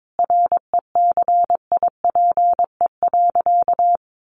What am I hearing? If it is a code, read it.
Morse code, heard as a keyed tone, about 22 wpm, 700 Hz. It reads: RECIPE.
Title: Sentences from Top 1200 Words 22wpm